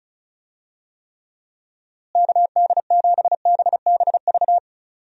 Morse code, heard as a keyed tone, about 36 wpm, 700 Hz. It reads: KD7BBV
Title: USA Call Signs - 6 Characters - Repeated 2x - 35wpm